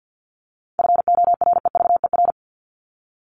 Morse code, heard as a keyed tone, about 50 wpm, 700 Hz. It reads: FOREVER
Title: Top 1501-1750 Words 50wpm